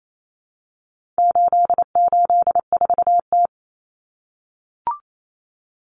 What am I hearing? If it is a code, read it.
Morse code, heard as a keyed tone, about 28 wpm, 700 Hz. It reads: 884T
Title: Common Cut Numbers - 4-Digit 28wpm